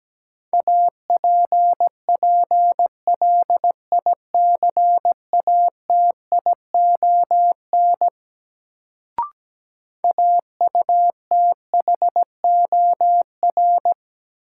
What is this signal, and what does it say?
Morse code, heard as a keyed tone, about 17 wpm, 700 Hz. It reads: APPLICATION AUTHOR
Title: Sentences from Top 1200 Words - New Word Drill Alt - 17wpm